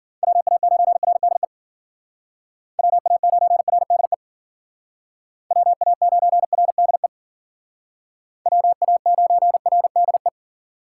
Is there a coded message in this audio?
Yes: Morse code from 0.2 to 10.3 s, heard as a keyed tone, about 51 wpm, 700 Hz. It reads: WA9RDE WA9RDE WA9RDE WA9RDE